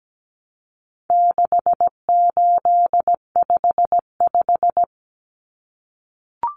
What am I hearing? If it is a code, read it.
Morse code, heard as a keyed tone, about 17 wpm, 700 Hz. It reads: 6855
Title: Common Cut Numbers - 4-Digit 17wpm